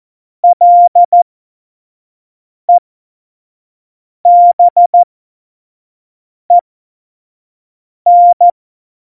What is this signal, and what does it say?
Morse code, heard as a keyed tone, about 14 wpm, spaced as for 5 wpm, 700 Hz. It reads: LEBEN
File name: Morsekurs / Witze / Witz 19. Januar 2021, 25 F70